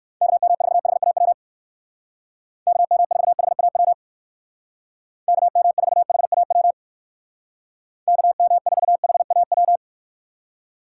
Morse code, heard as a keyed tone, about 64 wpm, 700 Hz. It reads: KM4HAW KM4HAW KM4HAW KM4HAW